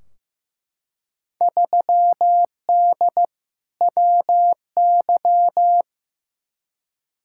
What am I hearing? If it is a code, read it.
Morse code, heard as a keyed tone, about 15 wpm, 700 Hz. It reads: 3D WY
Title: Field Day Exchanges Encore - 15wpm